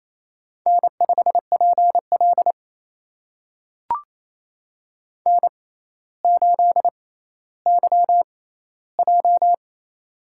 Morse code, heard as a keyed tone, about 28 wpm, spaced as for 14 wpm, 700 Hz. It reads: N5PL N8YJ
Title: USA Call Signs - 4 Characters - 6x Character Spacing - 28wpm